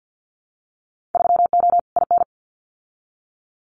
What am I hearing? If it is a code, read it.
Morse code, heard as a keyed tone, about 50 wpm, 700 Hz. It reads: 3O IN
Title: Winter Field Day Exchanges Encore 50wpm